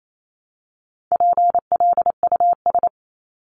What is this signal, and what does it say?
Morse code, heard as a keyed tone, about 28 wpm, 700 Hz. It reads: PLUS
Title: Top 1001-1250 Words 28wpm